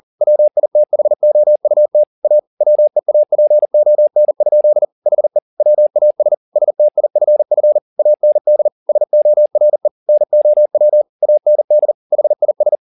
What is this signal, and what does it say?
Morse code, heard as a keyed tone, about 40 wpm, 600 Hz. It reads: WITHOUT A WEAPON? HE WAS STIFF AND SORE NOW AND HIS
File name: the old man and the sea 2 40wpm